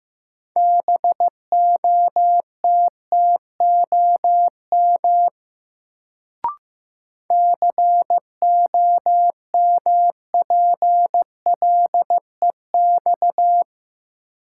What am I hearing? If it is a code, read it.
Morse code, heard as a keyed tone, about 15 wpm, 700 Hz. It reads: BOTTOM COMPLEX
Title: Sentences from Top 900 Words - New Word Drill - 15wpm